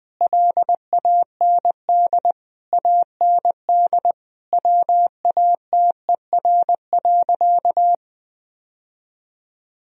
Morse code, heard as a keyed tone, about 20 wpm, 700 Hz. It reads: LAND AND WATER.
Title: Sentences from Top 1000 Words - Review Part 1 - 20wpm